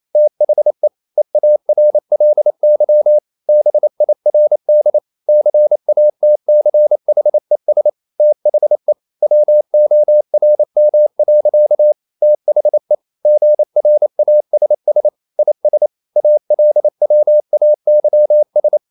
Morse code, heard as a keyed tone, about 28 wpm, 600 Hz. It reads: THE EARLY BIRD CATCHES THE WORM. THE GRASS IS ALWAYS
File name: CW / Practice / EnglishSayings28